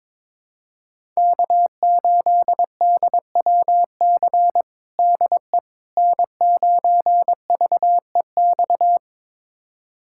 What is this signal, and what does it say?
Morse code, heard as a keyed tone, about 22 wpm, 700 Hz. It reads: K8DWC DE N9VEX
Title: USA Callsign DE Callsign - 5 Characters - 22wpm